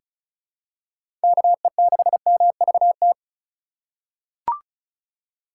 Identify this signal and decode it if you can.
Morse code, heard as a keyed tone, about 36 wpm, 700 Hz. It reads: KE6MVT
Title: USA Call Signs - 3x Character Spacing - 35wpm